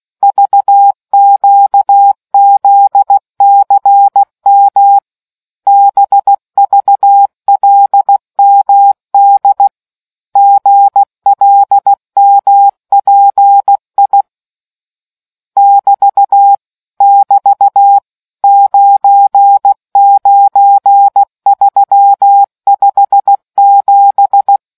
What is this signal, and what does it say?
Morse code, heard as a keyed tone, about 16 wpm, 800 Hz. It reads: VQZCM BVLMD GLMPI <BT> <BT> 99357